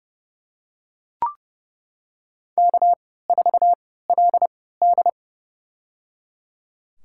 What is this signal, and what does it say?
Morse code, heard as a keyed tone, about 30 wpm, spaced as for 17 wpm, 700 Hz. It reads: K4LD
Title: USA Call Signs - 4 Characters - 3x Character Spacing - 30wpm